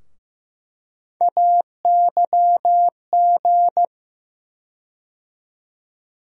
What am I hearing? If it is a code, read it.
Morse code, heard as a keyed tone, about 15 wpm, 700 Hz. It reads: AYG
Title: USA Call Sign Postfixes - 3 Letters 15wpm